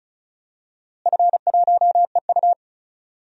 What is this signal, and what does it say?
Morse code, heard as a keyed tone, about 36 wpm, 690 Hz. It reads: F1EU
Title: DXCC-100 Call Signs - 4 Characters - Repeated 2x - 35wpm